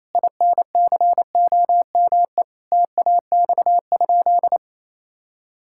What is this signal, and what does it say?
Morse code, heard as a keyed tone, about 28 wpm, 700 Hz. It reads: INCOME TAX?
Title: Sentences from Top 1000 Words - Review Part 5 - 28wpm